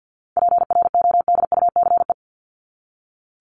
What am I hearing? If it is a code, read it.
Morse code, heard as a keyed tone, about 50 wpm, 700 Hz. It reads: PRODUCE
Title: ING Suffix - Top 100 Words with Base Words 50wpm